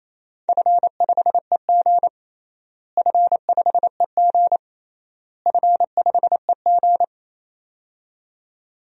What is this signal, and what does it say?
Morse code, heard as a keyed tone, about 28 wpm, 700 Hz. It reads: F5EG F5EG F5EG
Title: DXCC-100 Call Signs - 4 Characters - Repeated 3x - 28wpm